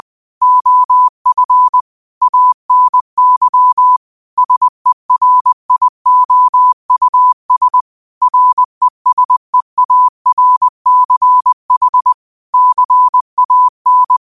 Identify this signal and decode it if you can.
Morse code, heard as a keyed tone, about 20 wpm, 1000 Hz. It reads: OF ANY SERIOUS RESEARCH CAN
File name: qotdfiles / 20wpm